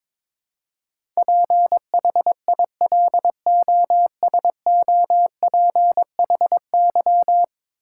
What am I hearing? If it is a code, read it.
Morse code, heard as a keyed tone, about 22 wpm, 700 Hz. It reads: PHILOSOPHY
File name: practice / 22wpm / 10 Letter Words - First 3 Letters Given - 22wpm